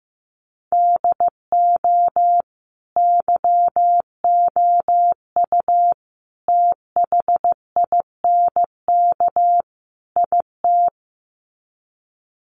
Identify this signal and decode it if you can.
Morse code, heard as a keyed tone, about 15 wpm, 700 Hz. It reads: DO YOU THINK IT
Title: Sets of 4 Words 15wpm